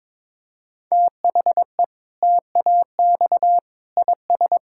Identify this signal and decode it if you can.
Morse code, heard as a keyed tone, about 21 wpm, 700 Hz. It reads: THE TAX IS
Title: Sentences from Top 1000 Words - Review Part 5 - 22wpm